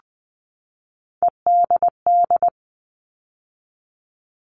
Morse code, heard as a keyed tone, about 20 wpm, 700 Hz. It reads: EDD
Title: All Cut Numbers - 3-Digit 20wpm